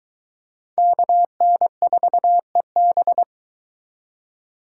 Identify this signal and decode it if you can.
Morse code, heard as a keyed tone, about 23 wpm, 700 Hz. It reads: KN4EB